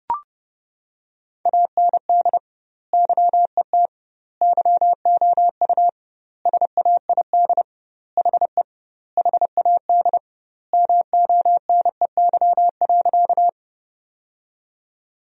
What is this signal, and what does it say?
Morse code, heard as a keyed tone, about 30 wpm, 700 Hz. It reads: AND YET YOU SAID HE HAD MONEY.
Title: 2x Word-Spacing - Sentences from Top 500 Words - 30wpm